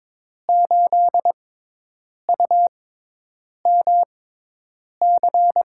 Morse code, heard as a keyed tone, about 21 wpm, spaced as for 8 wpm, 700 Hz. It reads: 8UMC
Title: USA Call Signs - 6 Characters - 6x Character Spacing - 22wpm